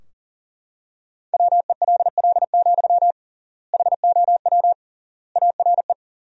Morse code, heard as a keyed tone, about 40 wpm, 700 Hz. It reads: WELL, HOW ARE
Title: 3x Word-Spacing - Sentences from Top 200 Words - 40wpm